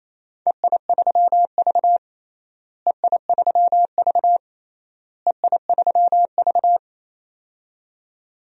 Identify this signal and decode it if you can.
Morse code, heard as a keyed tone, about 28 wpm, 700 Hz. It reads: EI3V EI3V EI3V
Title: DXCC-100 Call Signs - 4 Characters - Repeated 3x - 28wpm